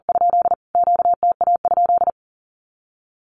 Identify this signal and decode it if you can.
Morse code, heard as a keyed tone, about 40 wpm, 700 Hz. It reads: ? QTA?
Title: Q-Codes Repeated 3x 40wpm